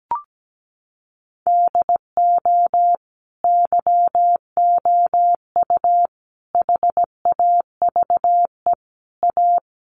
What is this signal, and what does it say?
Morse code, heard as a keyed tone, about 17 wpm, 700 Hz. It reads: DO YOU HAVE A
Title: Sentences from Top 1400 Words 17wpm